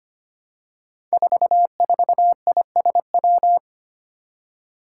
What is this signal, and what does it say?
Morse code, heard as a keyed tone, about 25 wpm, 700 Hz. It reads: 44ISW
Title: Five Challenging Letter-Number Combinations 25wpm